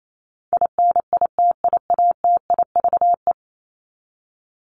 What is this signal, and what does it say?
Morse code, heard as a keyed tone, about 28 wpm, 700 Hz. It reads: INITIATIVE
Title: Top 1751-2000 Words 28wpm